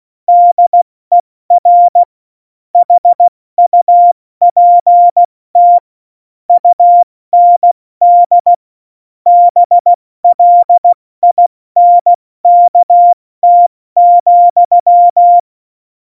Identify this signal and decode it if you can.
Morse code, heard as a keyed tone, about 16 wpm, 700 Hz. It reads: DER HUPT UND BLINKT,